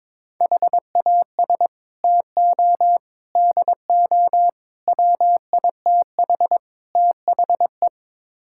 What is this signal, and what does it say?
Morse code, heard as a keyed tone, about 22 wpm, 700 Hz. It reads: HAS TO DO WITH THE
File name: practice / 22wpm / Build A Phrase - 5-Word Phrases - 22wpm